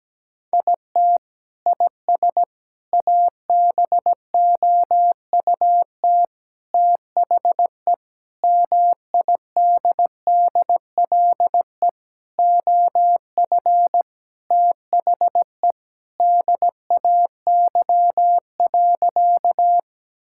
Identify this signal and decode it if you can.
Morse code, heard as a keyed tone, about 17 wpm, 700 Hz. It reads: IT IS ABOUT THE MIDDLE OF THE DAY.
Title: Sentences from Top 1000 Words - Review Part 6 - 17wpm